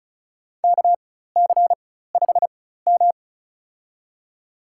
Morse code, heard as a keyed tone, about 36 wpm, spaced as for 16 wpm, 700 Hz. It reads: KC5M
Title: USA Call Signs - 4x Character Spacing - 35wpm